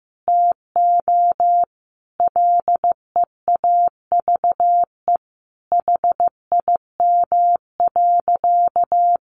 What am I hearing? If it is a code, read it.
Morse code, heard as a keyed tone, about 15 wpm, 700 Hz. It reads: TO LEAVE HIM.
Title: Sentences from Top 1500 Words 15wpm